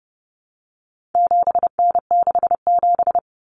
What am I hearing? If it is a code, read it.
Morse code, heard as a keyed tone, about 30 wpm, 700 Hz. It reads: 7N67
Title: Common Cut Numbers - 4-Digit 30wpm